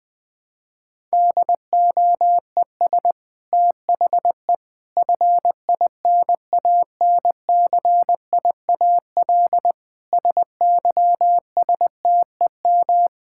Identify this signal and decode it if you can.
Morse code, heard as a keyed tone, about 20 wpm, 700 Hz. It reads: DOES THE FINANCIAL SYSTEM
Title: Sentences from Top 1000 Words - Review Part 5 - 20wpm